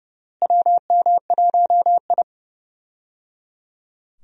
Morse code, heard as a keyed tone, about 29 wpm, 700 Hz. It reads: WM1I
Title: USA Call Signs - 4 Characters - Repeated 3x - 30wpm